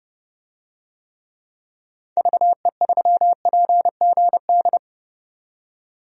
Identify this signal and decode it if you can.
Morse code, heard as a keyed tone, about 30 wpm, 700 Hz. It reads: VE3PGD
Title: DXCC-100 Call Signs - 6 Characters - 30wpm